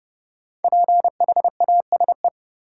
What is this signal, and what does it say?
Morse code, heard as a keyed tone, about 31 wpm, 700 Hz. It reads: PHASE
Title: Sentences from Top 1200 Words - New Word Drill Alt - 30wpm